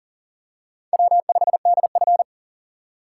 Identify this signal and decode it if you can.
Morse code, heard as a keyed tone, about 38 wpm, 690 Hz. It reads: W5DF
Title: USA Call Signs - 4 Characters - 2x Character Spacing - 40wpm